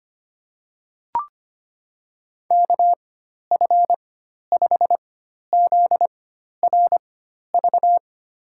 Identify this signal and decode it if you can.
Morse code, heard as a keyed tone, about 25 wpm, spaced as for 12 wpm, 700 Hz. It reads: KF5ZRV
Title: USA Call Signs - 6 Characters - 4x Character Spacing - 25wpm